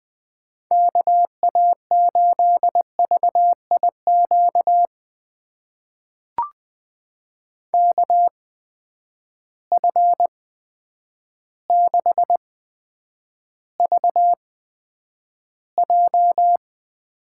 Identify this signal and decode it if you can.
Morse code, heard as a keyed tone, about 20 wpm, spaced as for 9 wpm, 700 Hz. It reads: KA8VIQ KF6VJ
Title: USA Call Signs - 6 Characters - 8x Character Spacing - 20wpm